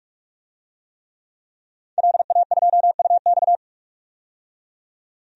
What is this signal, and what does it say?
Morse code, heard as a keyed tone, about 48 wpm, 700 Hz. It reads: LA2UX
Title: DXCC-100 Call Signs - 5 Characters - 45wpm